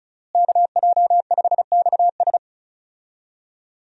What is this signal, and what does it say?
Morse code, heard as a keyed tone, about 36 wpm, 700 Hz. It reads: KJ5XS
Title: USA Call Signs - 5 Characters - Repeated 2x - 35wpm